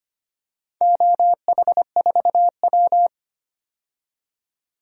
Morse code, heard as a keyed tone, about 25 wpm, 700 Hz. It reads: OH4W
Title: DXCC-100 Call Signs - 4 Characters - 25wpm